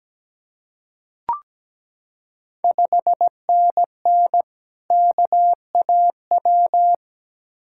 Morse code, heard as a keyed tone, about 17 wpm, 700 Hz. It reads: 5NN KAW
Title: Ontario QSO Party Exchange 17wpm